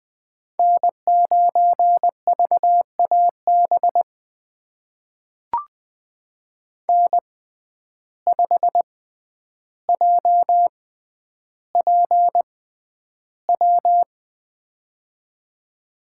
Morse code, heard as a keyed tone, about 20 wpm, spaced as for 10 wpm, 700 Hz. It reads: N9VAB N5JPW
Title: USA Call Signs - 5 Characters - 6x Character Spacing - 20wpm